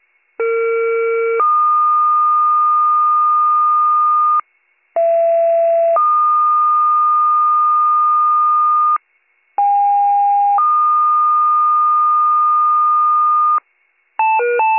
Two-tone page, heard as a one-second tone, then a three-second tone, about 460 then 1220 Hz, about 670 then 1220 Hz, about 810 then 1220 Hz. It is heard from 0.4 to 13.6 s.